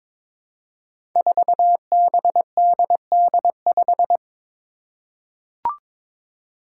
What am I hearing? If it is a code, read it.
Morse code, heard as a keyed tone, about 22 wpm, 700 Hz. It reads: 4BDD5
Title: Five Challenging Letter-Number Combinations 22wpm